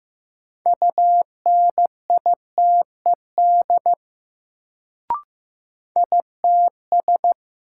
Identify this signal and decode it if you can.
Morse code, heard as a keyed tone, about 15 wpm, 700 Hz. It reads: UNITED ITS
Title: Sentences from Top 600 Words - New Word Drill - 15wpm